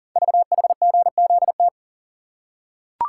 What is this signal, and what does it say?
Morse code, heard as a keyed tone, about 44 wpm, 700 Hz. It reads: VHGZT
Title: Code Groups - 5 Letters - 40wpm